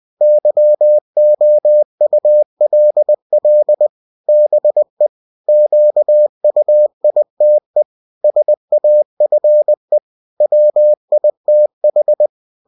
Morse code, heard as a keyed tone, about 20 wpm, 600 Hz. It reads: YOULL BE QUITE SAFE WITH
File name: cw-books / winnie-the-pooh / winnie-20wpm-page-0024